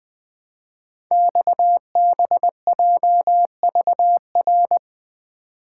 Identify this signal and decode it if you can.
Morse code, heard as a keyed tone, about 20 wpm, 700 Hz. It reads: XBJVR
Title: Code Groups - 5 Letters - 20wpm